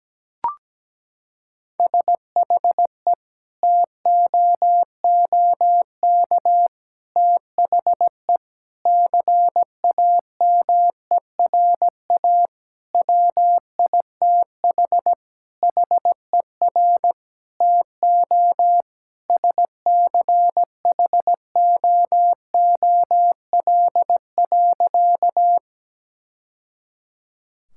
Morse code, heard as a keyed tone, about 17 wpm, 700 Hz. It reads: SHE TOOK THE CAMERA WITH HER TO SCHOOL.
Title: Sentences from Top 1000 Words - Review Part 5 - 17wpm